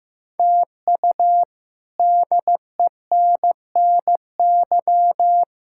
Morse code, heard as a keyed tone, about 15 wpm, 700 Hz. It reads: TU DENNY
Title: K1USN SST QSO Practice - 15wpm